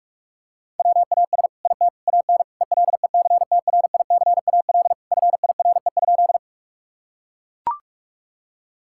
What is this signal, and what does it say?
Morse code, heard as a keyed tone, about 41 wpm, 700 Hz. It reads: WAS IT AN ELECTRICAL FIRE?
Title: Sentences from Top 1000 Words - Review Part 6 - 45wpm